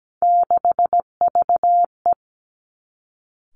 Morse code, heard as a keyed tone, about 17 wpm, 700 Hz. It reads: 6VE
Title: Three Challenging Letter-Number Combinations 17wpm